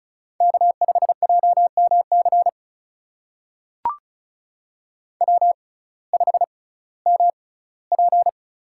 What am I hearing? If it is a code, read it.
Morse code, heard as a keyed tone, about 36 wpm, spaced as for 19 wpm, 700 Hz. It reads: K5JMC W5MP
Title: USA Call Signs - 6x Character Spacing - 35wpm